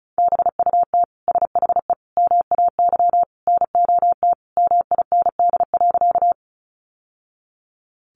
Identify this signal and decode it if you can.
Morse code, heard as a keyed tone, about 34 wpm, 700 Hz. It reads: BUT SHE MAY NOT MIND.
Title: Sentences from Top 600 Words 35wpm